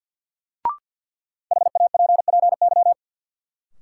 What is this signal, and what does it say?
Morse code, heard as a keyed tone, about 52 wpm, 700 Hz. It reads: HAPPY
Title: Sentences from Top 1000 Words - New Word Drill - 50wpm